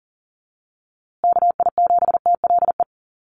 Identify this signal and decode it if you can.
Morse code, heard as a keyed tone, about 40 wpm, 700 Hz. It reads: KI7TLE
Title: USA Call Signs Repeated 2x 40wpm